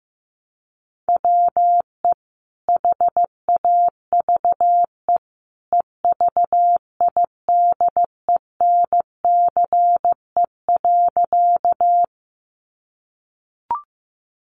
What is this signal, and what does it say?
Morse code, heard as a keyed tone, about 15 wpm, 700 Hz. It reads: WE HAVE EVIDENCE.
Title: Sentences from Top 1300 Words 15wpm